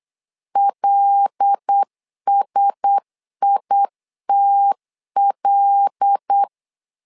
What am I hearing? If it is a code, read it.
Morse code, heard as a keyed tone, about 8 wpm, 800 Hz. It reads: LSITL
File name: LEZIONI / 5 LBVC / 5 L 8wpm 2831